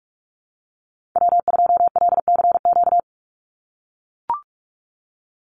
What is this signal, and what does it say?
Morse code, heard as a keyed tone, about 44 wpm, 700 Hz. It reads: W2LCQ
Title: Call Signs - 45wpm